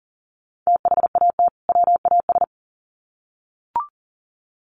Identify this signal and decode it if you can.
Morse code, heard as a keyed tone, about 40 wpm, 700 Hz. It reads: THAT WAS
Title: Sets of 2 Words 40wpm